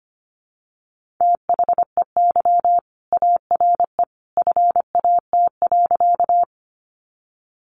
Morse code, heard as a keyed tone, about 25 wpm, 700 Hz. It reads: THEY ARE FAT.
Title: Sentences from Top 600 Words 25wpm